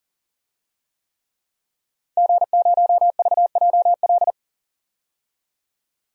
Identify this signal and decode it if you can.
Morse code, heard as a keyed tone, about 40 wpm, 700 Hz. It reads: G0VJL
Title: DXCC-100 Call Signs - 5 Characters - Repeated 3x - 40wpm